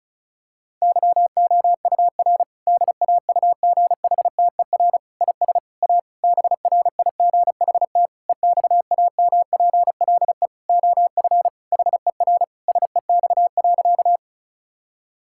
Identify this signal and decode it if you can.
Morse code, heard as a keyed tone, about 36 wpm, 700 Hz. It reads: YOUR DAUGHTER IS A BRIGHT EXAMPLE OF HER SEX.
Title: Sentences from Top 1600 Words 35wpm